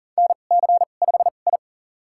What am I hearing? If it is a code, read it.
Morse code, heard as a keyed tone, about 40 wpm, spaced as for 29 wpm, 690 Hz. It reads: NC5I
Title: USA Call Signs - 4 Characters - 2x Character Spacing - 40wpm